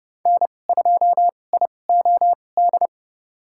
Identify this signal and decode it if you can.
Morse code, heard as a keyed tone, about 32 wpm, spaced as for 22 wpm, 700 Hz. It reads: N2IOD
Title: USA Call Signs - 5 Characters - 2x Character Spacing - 30wpm